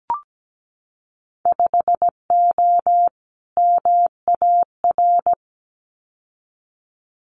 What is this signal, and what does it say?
Morse code, heard as a keyed tone, about 17 wpm, 700 Hz. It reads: 5O MAR
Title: Winter Field Day Exchanges Encore 17wpm